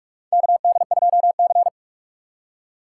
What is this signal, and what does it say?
Morse code, heard as a keyed tone, about 46 wpm, 700 Hz. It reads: KD2C
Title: USA Call Signs - 2x Character Spacing - 45wpm